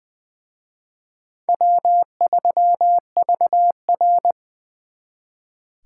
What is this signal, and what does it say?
Morse code, heard as a keyed tone, about 23 wpm, 700 Hz. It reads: W3VR
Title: Speed Racing - USA 1x2 Call Signs 20wpm